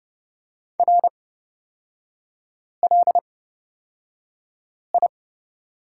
Morse code, heard as a keyed tone, about 30 wpm, spaced as for 5 wpm, 700 Hz. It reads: RLI
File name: Lesson 38e - Letter q - Three Characters 30wpm